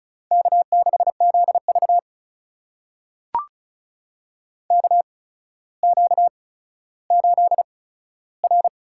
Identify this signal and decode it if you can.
Morse code, heard as a keyed tone, about 34 wpm, spaced as for 15 wpm, 700 Hz. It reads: K6ZV KQ8R
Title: USA Call Signs - 4 Characters - 8x Character Spacing - 35wpm